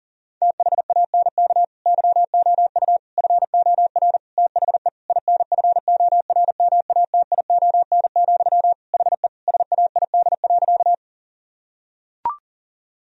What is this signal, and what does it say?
Morse code, heard as a keyed tone, about 37 wpm, 700 Hz. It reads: THANK YOU FOR THE INFORMATION, HE SAID.
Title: Sentences from Top 1000 Words - Review Part 3 - 40wpm